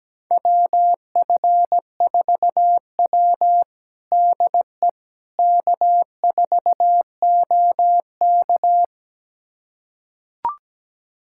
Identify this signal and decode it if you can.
Morse code, heard as a keyed tone, about 17 wpm, 700 Hz. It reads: WF4W DE K4OK
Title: USA Callsign DE Callsign - 4 Characters - 17wpm